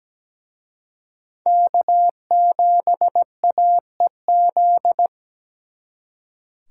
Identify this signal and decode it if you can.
Morse code, heard as a keyed tone, about 20 wpm, 700 Hz. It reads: K7AEZ
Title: Speed Racing - USA Call Signs 17wpm